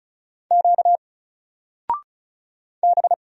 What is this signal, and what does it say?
Morse code, heard as a keyed tone, about 34 wpm, 700 Hz. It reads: Q B
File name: Single-Letter-Number-35wpm